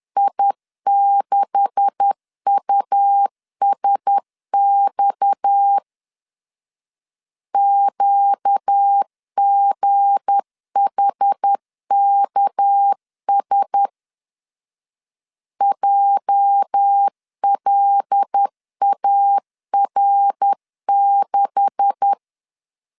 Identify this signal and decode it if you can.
Morse code, heard as a keyed tone, about 10 wpm, 800 Hz. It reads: I6USX QGHKS JLAR6
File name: LEZIONI / 9 286 10wpm 2735